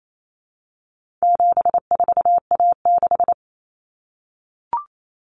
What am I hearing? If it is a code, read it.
Morse code, heard as a keyed tone, about 28 wpm, 700 Hz. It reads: 74A6
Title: Common Cut Numbers - 4-Digit 28wpm